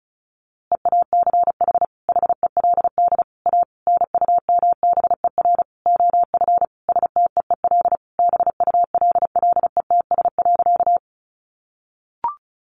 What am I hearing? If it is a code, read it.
Morse code, heard as a keyed tone, about 34 wpm, 700 Hz. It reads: EACH HELD A NUMBER OF STEEL BULLETS.
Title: A Sound Of Thunder 35wpm